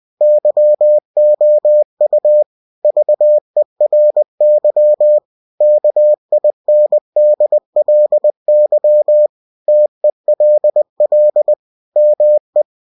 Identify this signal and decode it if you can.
Morse code, heard as a keyed tone, about 20 wpm, 600 Hz. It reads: YOU VERY KINDLY TELL ME
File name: cw-books / winnie-the-pooh / winnie-20wpm-page-0029